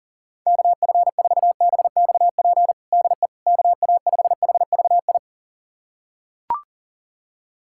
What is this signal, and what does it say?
Morse code, heard as a keyed tone, about 40 wpm, 700 Hz. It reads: KF4BXP DE KA5HVI
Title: USA Callsign DE Callsign - 6 Characters - 40wpm